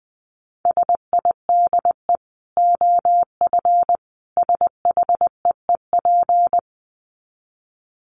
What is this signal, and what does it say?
Morse code, heard as a keyed tone, about 20 wpm, 700 Hz. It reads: SIDE OF SHEEP
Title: Dit Challenge 20wpm